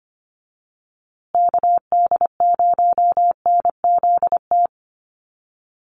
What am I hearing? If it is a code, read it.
Morse code, heard as a keyed tone, about 25 wpm, 700 Hz. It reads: KD0NZT
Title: USA Call Signs Repeated 3x 25wpm